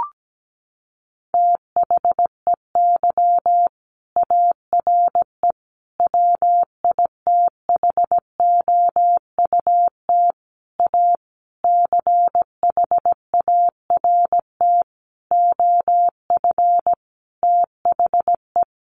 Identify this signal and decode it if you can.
Morse code, heard as a keyed tone, about 17 wpm, 700 Hz. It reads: THEY ARE WITHOUT A CHART OF THE
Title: Sentences from Top 1300 Words 17wpm